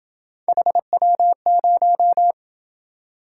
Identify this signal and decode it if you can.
Morse code, heard as a keyed tone, about 26 wpm, 700 Hz. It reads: HW0